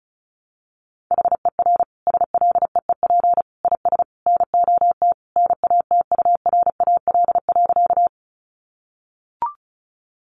Morse code, heard as a keyed tone, about 34 wpm, 700 Hz. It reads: HER SLEEP IS NOT NATURAL.
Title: Sentences from Top 700 Words 35wpm